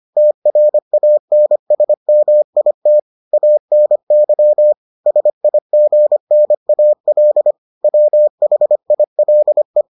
Morse code, heard as a keyed tone, about 25 wpm, 600 Hz. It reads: TRANSMIT ANY SIGNAL WHILE